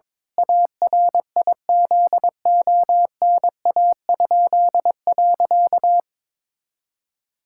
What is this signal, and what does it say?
Morse code, heard as a keyed tone, about 22 wpm, 700 Hz. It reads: ARIZONA?.
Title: Pangram Challenge v2 - 22wpm